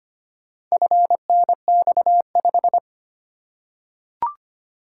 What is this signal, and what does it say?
Morse code, heard as a keyed tone, about 25 wpm, 700 Hz. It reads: FNX5
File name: practice / 25wpm / Four Letter-Number Combinations - 25wpm